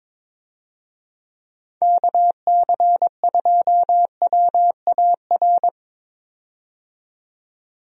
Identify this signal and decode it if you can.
Morse code, heard as a keyed tone, about 22 wpm, 700 Hz. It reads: KC2WAR
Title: USA Call Signs - 6 Characters - 22wpm